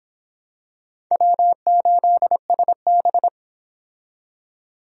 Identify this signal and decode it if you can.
Morse code, heard as a keyed tone, about 26 wpm, 700 Hz. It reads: W8SB